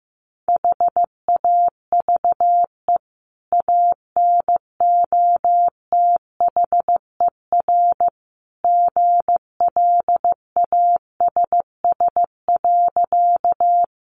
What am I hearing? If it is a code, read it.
Morse code, heard as a keyed tone, about 15 wpm, 700 Hz. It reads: HAVE ANOTHER GLASS.
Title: Sentences from Top 1300 Words 15wpm